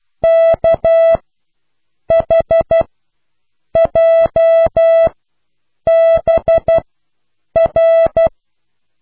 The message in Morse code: KHJBR